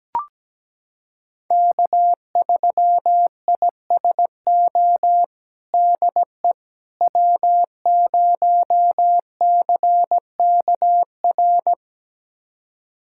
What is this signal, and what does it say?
Morse code, heard as a keyed tone, about 17 wpm, 700 Hz. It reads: K3ISO DE W0CKR
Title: USA Callsign DE Callsign - 5 Characters - 17wpm